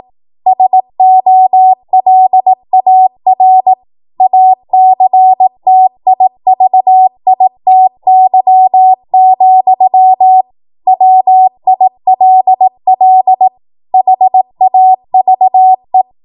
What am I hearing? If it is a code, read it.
Morse code, heard as a keyed tone, about 18 wpm, 750 Hz. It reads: SOLAR ACTIVITY, WILL HAVE
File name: Morse / 18WPM